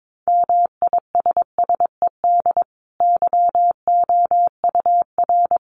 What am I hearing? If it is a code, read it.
Morse code, heard as a keyed tone, about 22 wpm, 700 Hz. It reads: MISSED YOUR
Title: Common QSO Phrases 22wpm